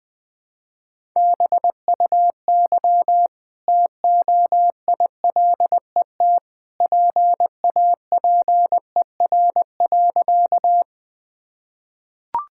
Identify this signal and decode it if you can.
Morse code, heard as a keyed tone, about 20 wpm, 700 Hz. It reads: BUY TOILET PAPER.
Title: Easy Sentences with Hints - 20wpm